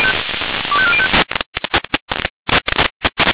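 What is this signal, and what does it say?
Five-tone selective call (ZVEI), heard as quick runs of five tones, tones of about 70 ms. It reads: 25405
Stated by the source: ZVEI five-tone call 25218 received on SDR.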